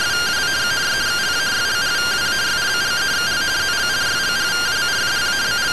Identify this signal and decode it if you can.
Two-tone teleprinter signal, about 170 Hz shift.